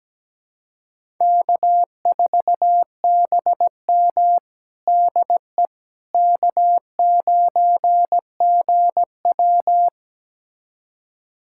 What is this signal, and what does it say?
Morse code, heard as a keyed tone, about 17 wpm, 700 Hz. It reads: K4BM DE K9GW
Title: USA Callsign DE Callsign - 4 Characters - 17wpm